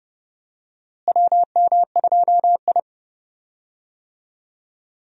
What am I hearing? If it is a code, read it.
Morse code, heard as a keyed tone, about 32 wpm, 700 Hz. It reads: WM2I
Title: USA Call Signs - 4 Characters - Repeated 2x - 30wpm